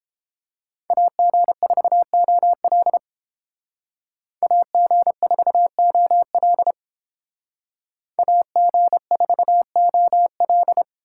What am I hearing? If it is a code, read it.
Morse code, heard as a keyed tone, about 29 wpm, 700 Hz. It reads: AG4OL AG4OL AG4OL